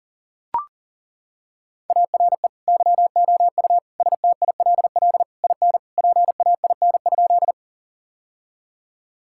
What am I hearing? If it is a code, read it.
Morse code, heard as a keyed tone, about 40 wpm, 700 Hz. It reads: ARE YOU STILL IN PAIN?
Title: Sentences from Top 1600 Words 40wpm